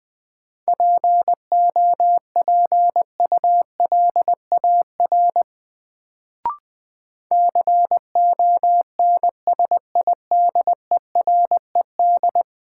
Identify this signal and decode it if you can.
Morse code, heard as a keyed tone, about 20 wpm, 700 Hz. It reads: POPULAR CONSIDERED
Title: Sets of 5 Words - New Word Drill - 20wpm